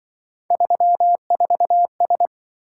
Morse code, heard as a keyed tone, about 24 wpm, 700 Hz. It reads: 34S
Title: Speed-Racing - Three Letter-Number Combinations - 20wpm